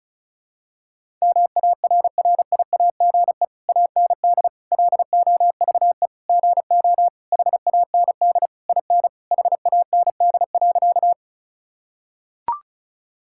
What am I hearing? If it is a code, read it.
Morse code, heard as a keyed tone, about 36 wpm, 700 Hz. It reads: MARRIAGE AND LOVE GO HAND IN HAND.
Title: Sentences from Top 1000 Words - Review Part 5 - 35wpm